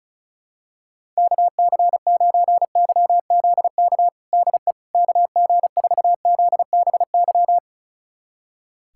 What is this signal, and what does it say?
Morse code, heard as a keyed tone, about 34 wpm, 700 Hz. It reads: KC9YZK DE KG4ZBY
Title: USA Callsign DE Callsign 35wpm